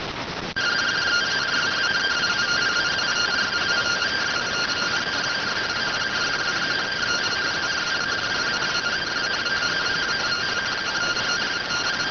Two-tone teleprinter signal, about 170 Hz shift.